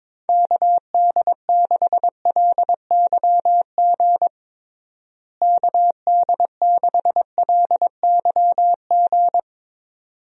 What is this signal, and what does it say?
Morse code, heard as a keyed tone, about 22 wpm, 700 Hz. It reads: KD6LYG KD6LYG
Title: USA Call Signs - 6 Characters - Repeated 2x - 22wpm